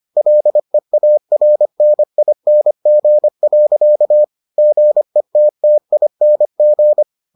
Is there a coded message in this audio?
Yes: Morse code from 0.2 to 7.0 s, heard as a keyed tone, about 25 wpm, 600 Hz. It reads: LEARNING. GETTING